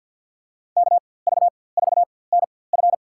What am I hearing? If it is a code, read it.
Morse code, heard as a keyed tone, about 48 wpm, spaced as for 23 wpm, 700 Hz. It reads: KV4NF
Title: USA Call Signs - 5 Characters - 4x Character Spacing - 50wpm